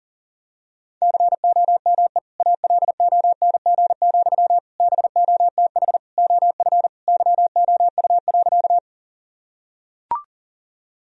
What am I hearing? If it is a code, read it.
Morse code, heard as a keyed tone, about 40 wpm, 700 Hz. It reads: COME ALONG, BOTH OF YOU.
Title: Sentences from Top 1000 Words - Review Part 1 - 40wpm